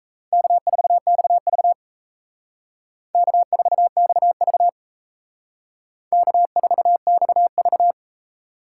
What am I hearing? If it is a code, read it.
Morse code, heard as a keyed tone, about 34 wpm, 700 Hz. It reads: K4XV K4XV K4XV
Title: Speed Racing - USA 1x2 Call Signs 28wpm